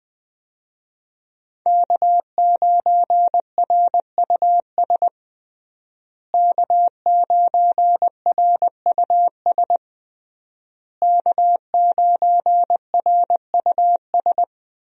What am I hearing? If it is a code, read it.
Morse code, heard as a keyed tone, about 21 wpm, 700 Hz. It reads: K9RUS K9RUS K9RUS
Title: USA Call Signs - 5 Characters - Repeated 3x - 20wpm